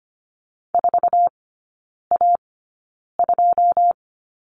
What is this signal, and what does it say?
Morse code, heard as a keyed tone, about 25 wpm, spaced as for 9 wpm, 690 Hz. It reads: 4A2
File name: Lesson 15e - Number 5 - Three Characters 25wpm